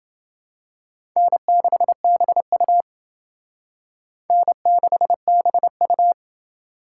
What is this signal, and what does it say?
Morse code, heard as a keyed tone, about 29 wpm, 700 Hz. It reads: N6BU N6BU